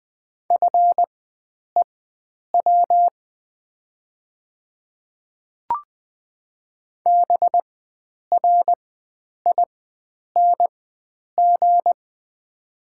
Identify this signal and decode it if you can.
Morse code, heard as a keyed tone, about 20 wpm, spaced as for 9 wpm, 700 Hz. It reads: FEW BRING
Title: 4x Character Spacing - Top 500 Words - 20wpm